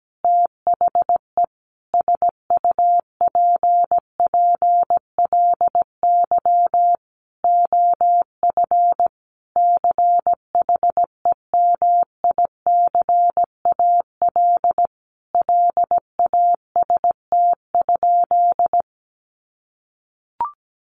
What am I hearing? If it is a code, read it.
Morse code, heard as a keyed tone, about 17 wpm, 700 Hz. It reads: THE SUPPLY OF CHEMICAL LAST?
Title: Sentences from Top 1200 Words 17wpm